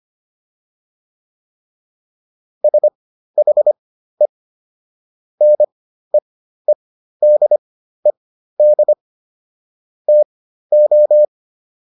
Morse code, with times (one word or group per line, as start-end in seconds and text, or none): 2.6-4.3 SHE
5.4-8.9 NEEDED
10.1-11.3 TO